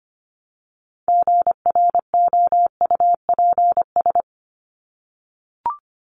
Morse code, heard as a keyed tone, about 25 wpm, 700 Hz. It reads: GROUPS
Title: ES:S Suffix - Top 100 Words with Base Words 25wpm